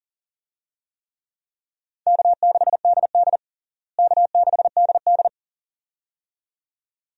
Morse code, heard as a keyed tone, about 37 wpm, 700 Hz. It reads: K6DD K6DD
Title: USA Call Signs - 4 Characters - Repeated 2x - 40wpm